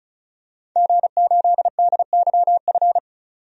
Morse code, heard as a keyed tone, about 34 wpm, 700 Hz. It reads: G8DYF
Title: DXCC-100 Call Signs - 5 Characters - 35wpm